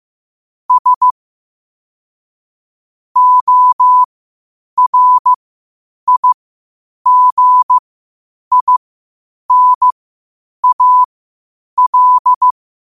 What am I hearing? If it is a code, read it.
Morse code, heard as a keyed tone, about 15 wpm, spaced as for 8 wpm, 1000 Hz. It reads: S ORIGINAL
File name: qotdfiles / 8wpm